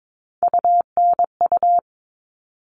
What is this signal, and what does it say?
Morse code, heard as a keyed tone, about 22 wpm, 700 Hz. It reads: UNU
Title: All Cut Numbers - 3-Digit 22wpm